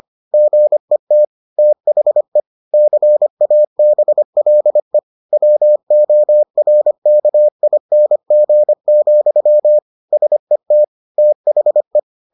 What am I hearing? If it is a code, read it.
Morse code, heard as a keyed tone, about 25 wpm, 600 Hz. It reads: GET THE CABLE WORKING, SET THE